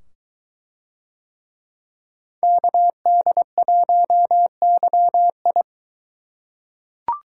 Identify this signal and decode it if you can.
Morse code, heard as a keyed tone, about 21 wpm, 700 Hz. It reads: KD1YI